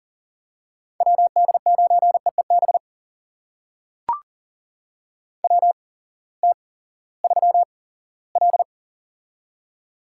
Morse code, heard as a keyed tone, about 40 wpm, spaced as for 20 wpm, 700 Hz. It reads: WD9EEB WT3L
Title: USA Call Signs - 8x Character Spacing - 40wpm